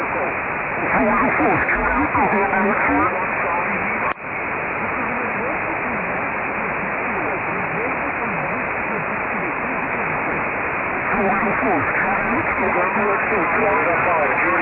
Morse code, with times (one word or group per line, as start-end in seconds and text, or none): none